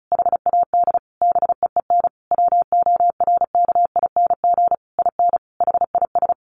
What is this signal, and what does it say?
Morse code, heard as a keyed tone, about 34 wpm, 700 Hz. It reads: HAD BEEN WORKING IN HIS
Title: Sentences from Top 1200 Words 35wpm